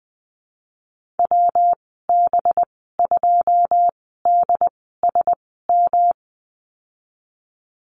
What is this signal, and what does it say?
Morse code, heard as a keyed tone, about 20 wpm, spaced as for 15 wpm, 700 Hz. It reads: WB2DSM
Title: 186 Lesson 30g - Letter b - Call Signs 20wpm